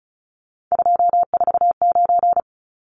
Morse code, heard as a keyed tone, about 34 wpm, 700 Hz. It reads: 249
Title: Numbers - 3-Digit 35wpm